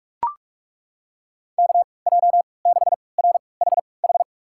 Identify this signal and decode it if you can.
Morse code, heard as a keyed tone, about 41 wpm, spaced as for 25 wpm, 700 Hz. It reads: KJ6RHH
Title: USA Call Signs - 6 Characters - 3x Character Spacing - 45wpm